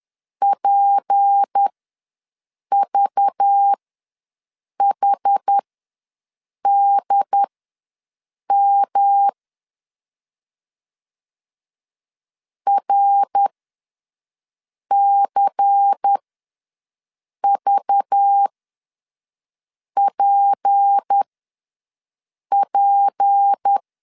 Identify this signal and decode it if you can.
Morse code, heard as a keyed tone, about 10 wpm, spaced as for 5 wpm, 800 Hz. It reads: PVHDM RCVPP
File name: LEZIONI / 6 PKX0 / 6 P ws 10wpm 3751